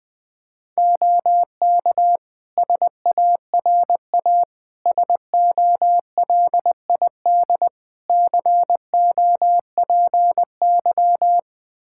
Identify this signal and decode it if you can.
Morse code, heard as a keyed tone, about 20 wpm, 700 Hz. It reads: OK SARA SOLID COPY
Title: Practice QSO - Exchange 3 - More Info - 20wpm